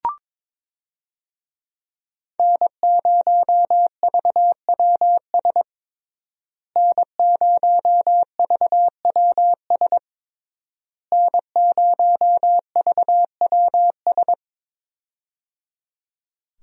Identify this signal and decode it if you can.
Morse code, heard as a keyed tone, about 22 wpm, 700 Hz. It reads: N0VWS N0VWS N0VWS
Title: USA Call Signs - 5 Characters - Repeated 3x - 22wpm